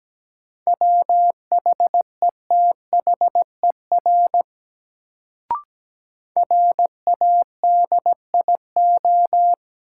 Morse code, heard as a keyed tone, about 17 wpm, 700 Hz. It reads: WHETHER RADIO
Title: Sentences from Top 1000 Words - Review Part 3 - New Word Drill - 17wpm